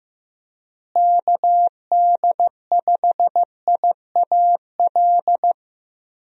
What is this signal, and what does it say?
Morse code, heard as a keyed tone, about 15 wpm, 700 Hz. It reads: KD5IAL
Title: USA Call Signs - 6 Characters - Repeated 2x - 15wpm